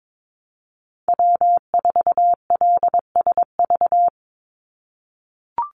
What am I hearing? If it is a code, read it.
Morse code, heard as a keyed tone, about 22 wpm, 700 Hz. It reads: W4LSV
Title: Call Signs - 22wpm